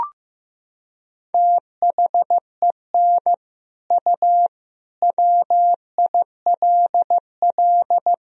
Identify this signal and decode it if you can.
Morse code, heard as a keyed tone, about 16 wpm, 700 Hz. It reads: THEN U WILL
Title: Sentences from Top 400 Words and Abbreviations 15wpm